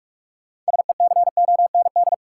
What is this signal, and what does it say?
Morse code, heard as a keyed tone, about 46 wpm, 700 Hz. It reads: SECOND
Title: Sentences from Top 300 Words - New Word Drill - 45wpm